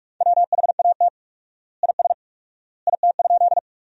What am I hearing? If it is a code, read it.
Morse code, heard as a keyed tone, about 46 wpm, 700 Hz. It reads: WHAT IS IT?
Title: 4x Word-Spacing - Sentences from Top 100 Words - 45wpm